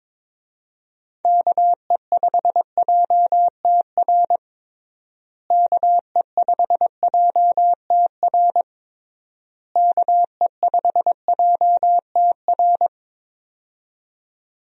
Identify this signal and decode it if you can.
Morse code, heard as a keyed tone, about 22 wpm, 700 Hz. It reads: KE5JTR KE5JTR KE5JTR
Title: USA Call Signs - 6 Characters - Repeated 3x - 22wpm